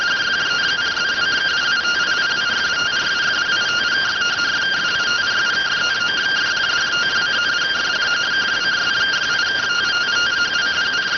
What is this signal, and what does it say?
Two-tone teleprinter signal, about 170 Hz shift.